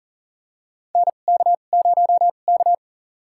Morse code, heard as a keyed tone, about 40 wpm, spaced as for 29 wpm, 700 Hz. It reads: NK0K
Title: USA Call Signs - 2x Character Spacing - 40wpm